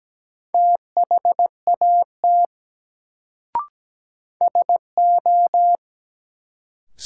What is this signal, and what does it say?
Morse code, heard as a keyed tone, about 17 wpm, 700 Hz. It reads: THAT SO
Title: Sentences from Top 100 Words - New Word Drill - 17wpm